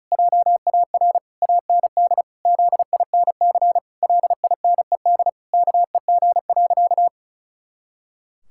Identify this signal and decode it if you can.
Morse code, heard as a keyed tone, about 34 wpm, 700 Hz. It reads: JAR AND ZINC LINED KEG.
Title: Pangram Challenge v2 - 35wpm